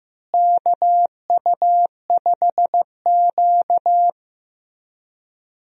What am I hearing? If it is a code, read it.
Morse code, heard as a keyed tone, about 15 wpm, 700 Hz. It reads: KU5Q
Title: USA Call Signs - 4 Characters - Repeated 3x - 15wpm